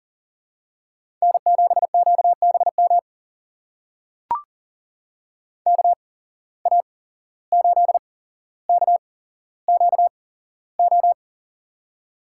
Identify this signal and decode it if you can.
Morse code, heard as a keyed tone, about 40 wpm, spaced as for 16 wpm, 700 Hz. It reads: N7QBM KA8KQO
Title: USA Call Signs - 8x Character Spacing - 40wpm